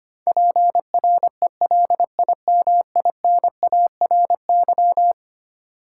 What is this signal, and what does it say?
Morse code, heard as a keyed tone, about 25 wpm, 700 Hz. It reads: PRELIMINARY
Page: morse-code-ninja-practice-25wpm